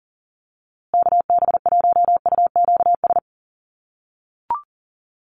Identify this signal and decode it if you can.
Morse code, heard as a keyed tone, about 40 wpm, 700 Hz. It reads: KB1UQS
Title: USA Call Signs Repeated 3x 40wpm